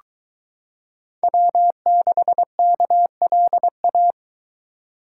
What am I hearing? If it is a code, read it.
Morse code, heard as a keyed tone, about 22 wpm, 700 Hz. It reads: W6KLA